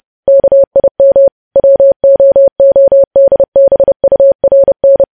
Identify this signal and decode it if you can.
Morse code, heard as a keyed tone, about 30 wpm, 550 Hz. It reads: KIM WOODBURN
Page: news headlines in morse code at 30 wpm